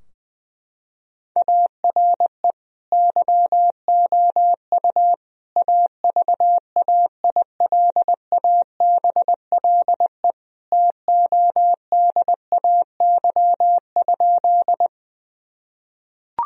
Morse code, heard as a keyed tone, about 21 wpm, 700 Hz. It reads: ARE YOU AVAILABLE TODAY?
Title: Sentences from Top 1000 Words - Review Part 3 - 20wpm